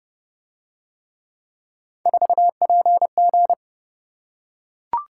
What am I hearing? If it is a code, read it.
Morse code, heard as a keyed tone, about 27 wpm, 700 Hz. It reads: 4PG